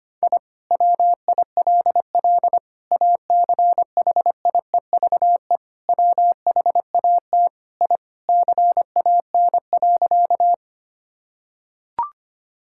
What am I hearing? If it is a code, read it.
Morse code, heard as a keyed tone, about 25 wpm, 700 Hz. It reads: I WILL ACHIEVE WHAT I CAN.
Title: Sentences from Top 1000 Words - Review Part 6 - 25wpm